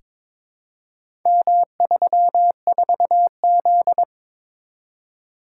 Morse code, heard as a keyed tone, about 22 wpm, 700 Hz. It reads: M34Z
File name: practice / 22wpm / Four Letter-Number Combinations - 22wpm